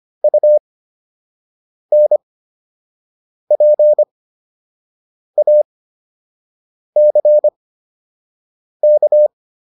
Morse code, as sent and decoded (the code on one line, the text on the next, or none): ..- -. .--. .- -.-. -.-
UNPACK